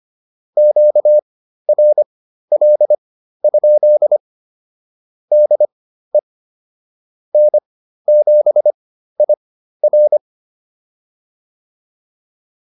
Morse code, heard as a keyed tone, about 25 wpm, spaced as for 13 wpm, 600 Hz. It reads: QRL? DE N7IR